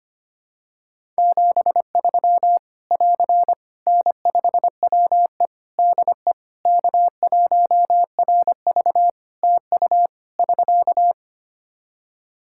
Morse code, heard as a keyed tone, about 25 wpm, 700 Hz. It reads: 73 <AR> N5WE DE K1RV TU <SK>
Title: Practice QSO - Exchange 4 - Ending - 25wpm